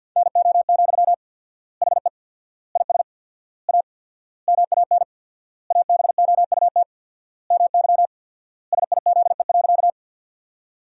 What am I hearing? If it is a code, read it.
Morse code, heard as a keyed tone, about 50 wpm, 700 Hz. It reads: NO, HE IS A MAN ABOUT MY SIZE.
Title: 4x Word-Spacing - Sentences from Top 400 Words - 50wpm